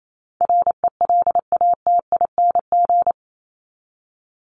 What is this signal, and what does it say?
Morse code, heard as a keyed tone, about 28 wpm, 700 Hz. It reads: RELATING
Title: ING Suffix - Top 100 Words with Base Words 28wpm